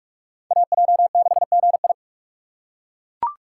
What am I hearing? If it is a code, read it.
Morse code, heard as a keyed tone, about 42 wpm, 700 Hz. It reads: AJ6GI
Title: USA Call Signs - 5 Characters - 3x Character Spacing - 45wpm